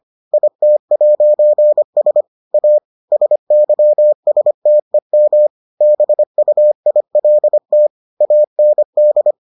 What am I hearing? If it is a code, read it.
Morse code, heard as a keyed tone, about 25 wpm, 600 Hz. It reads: IT'S A SYSTEM BUILT AND